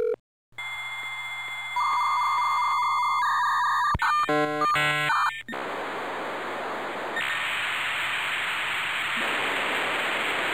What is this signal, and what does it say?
Dial-up modem handshake.